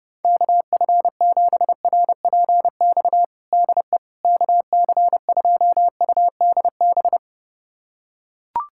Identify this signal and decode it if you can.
Morse code, heard as a keyed tone, about 30 wpm, 700 Hz. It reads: KF7RPX DE KC2UDB
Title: USA Callsign DE Callsign - 6 Characters - 30wpm